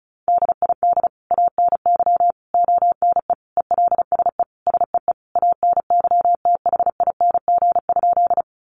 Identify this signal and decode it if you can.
Morse code, heard as a keyed tone, about 34 wpm, 700 Hz. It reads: DID ANY ONE ELSE SEE ANYTHING?
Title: Sentences from Top 1000 Words 35wpm